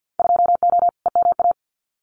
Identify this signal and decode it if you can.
Morse code, heard as a keyed tone, about 50 wpm, 700 Hz. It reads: 2O EMA
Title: Winter Field Day Exchanges 50wpm